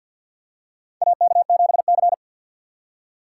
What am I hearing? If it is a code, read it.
Morse code, heard as a keyed tone, about 52 wpm, 700 Hz. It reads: AK7C
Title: USA Call Signs - 6x Character Spacing - 50wpm